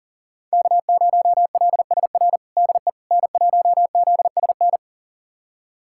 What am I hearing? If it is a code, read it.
Morse code, heard as a keyed tone, about 40 wpm, 700 Hz. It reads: K0LSR DE N1ZSN
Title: USA Callsign DE Callsign - 5 Characters - 40wpm